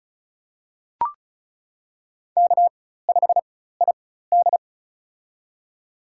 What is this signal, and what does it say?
Morse code, heard as a keyed tone, about 36 wpm, spaced as for 16 wpm, 700 Hz. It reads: K5ID
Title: USA Call Signs - 4 Characters - 4x Character Spacing - 35wpm